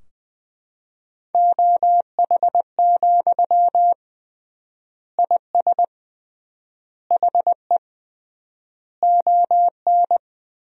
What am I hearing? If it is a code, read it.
Morse code, heard as a keyed tone, about 20 wpm, 700 Hz. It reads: OH, IS HE ON
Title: 3x Word-Spacing - Sentences from Top 500 Words - 20wpm